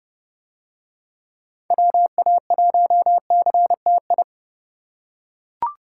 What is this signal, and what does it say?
Morse code, heard as a keyed tone, about 29 wpm, 700 Hz. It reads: WA1CTI